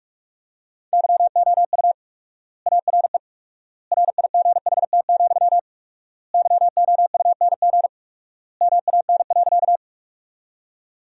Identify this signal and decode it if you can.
Morse code, heard as a keyed tone, about 46 wpm, 700 Hz. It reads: YOU ARE RIGHT, YOUNG MAN.
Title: 4x Word-Spacing - Sentences from Top 400 Words - 45wpm